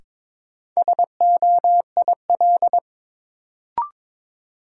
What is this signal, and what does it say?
Morse code, heard as a keyed tone, about 22 wpm, 700 Hz. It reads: SOIL
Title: Sentences from Top 1500 Words - New Word Drill Alt - 22wpm